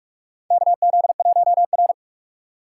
Morse code, heard as a keyed tone, about 46 wpm, 700 Hz. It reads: KZ1R
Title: USA Call Signs - 2x Character Spacing - 45wpm